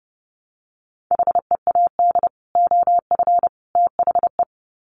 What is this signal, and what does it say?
Morse code, heard as a keyed tone, about 28 wpm, 700 Hz. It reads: HEAD OF THE
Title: Sets of 3 Words 30wpm